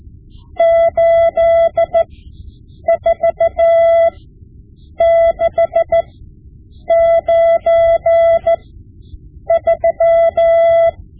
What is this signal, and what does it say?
Morse code, heard as a keyed tone, about 14 wpm, spaced as for 7 wpm, 670 Hz. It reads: 84693
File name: Morse Lessons / Morse Lessons / Tape-5B